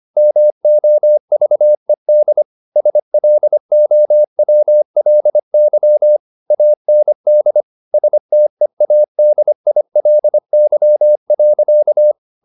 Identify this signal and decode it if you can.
Morse code, heard as a keyed tone, about 25 wpm, 600 Hz. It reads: MOVED SLOWLY AND STEADILY.
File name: the old man and the sea part 1 25wpm